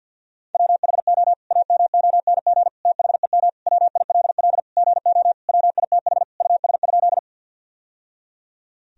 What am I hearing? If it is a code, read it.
Morse code, heard as a keyed tone, about 52 wpm, 700 Hz. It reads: WHO AMONG THEM WILL GO WITH US?
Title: Sentences from Top 1000 Words - Review Part 3 - 50wpm